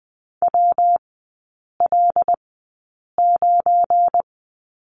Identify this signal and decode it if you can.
Morse code, heard as a keyed tone, about 20 wpm, spaced as for 8 wpm, 690 Hz. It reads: WL9
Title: 172 Lesson 28e - Number 9 - Three Characters 20wpm